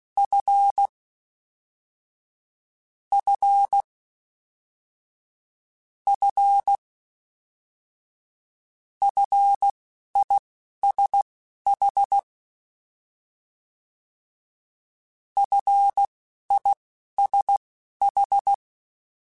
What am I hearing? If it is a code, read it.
Morse code, heard as a keyed tone, about 16 wpm, spaced as for 11 wpm, 800 Hz. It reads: F F F FISH FISH
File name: Short Runs / Short Runs-mp3s / SinglePrefixes @10